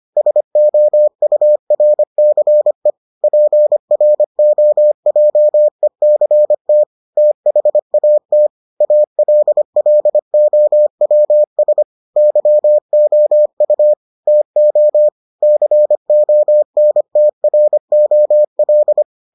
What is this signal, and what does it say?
Morse code, heard as a keyed tone, about 25 wpm, 600 Hz. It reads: SOURCE PROJECT THAT ALLOWS YOU TO CONTROL